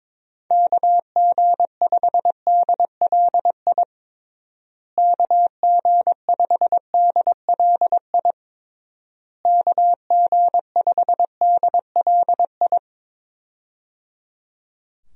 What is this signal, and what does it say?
Morse code, heard as a keyed tone, about 22 wpm, 700 Hz. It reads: KG5DLI KG5DLI KG5DLI
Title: USA Call Signs - 6 Characters - Repeated 3x - 22wpm